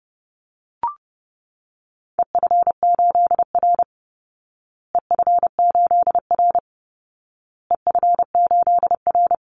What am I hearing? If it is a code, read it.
Morse code, heard as a keyed tone, about 28 wpm, 700 Hz. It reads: EF8R EF8R EF8R
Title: Call Signs Repeated 2x - 30wpm